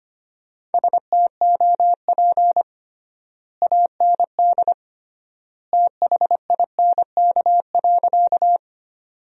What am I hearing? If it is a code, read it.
Morse code, heard as a keyed tone, about 25 wpm, 700 Hz. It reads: STOP AND THINK.
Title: 3x Word-Spacing - Sentences from Top 300 Words - 25wpm